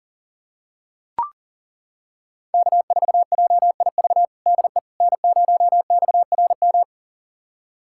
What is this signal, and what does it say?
Morse code, heard as a keyed tone, about 40 wpm, 700 Hz. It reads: K4JIV DE N0XRM
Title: USA Callsign DE Callsign - 5 Characters - 40wpm